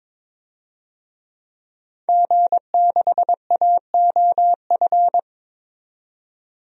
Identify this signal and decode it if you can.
Morse code, heard as a keyed tone, about 22 wpm, 700 Hz. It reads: G6AOF
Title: DXCC-100 Call Signs - 5 Characters - 22wpm